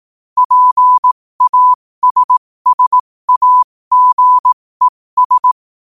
Morse code, heard as a keyed tone, about 18 wpm, spaced as for 15 wpm, 1000 Hz. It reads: PASSAGES
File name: qotdfiles / 15wpm